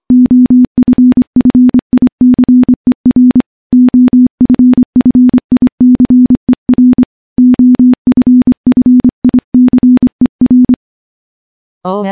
Morse code, heard as a keyed tone, about 25 wpm, 250 Hz. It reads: OFFICER OFFICER OFFICER
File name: morse-training-25WPM